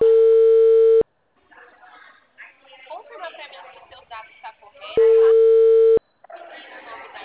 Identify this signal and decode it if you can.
Ringback tone (China, 450 Hz), with speech.
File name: out- -2021- - - .86567